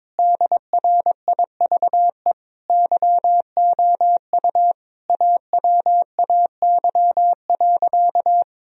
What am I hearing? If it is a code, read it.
Morse code, heard as a keyed tone, about 22 wpm, 700 Hz. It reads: DRIVE YOU AWAY.
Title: Sentences from Top 1000 Words - Review Part 2 - 22wpm